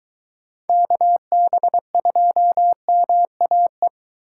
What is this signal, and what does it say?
Morse code, heard as a keyed tone, about 20 wpm, 700 Hz. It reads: KB2MAE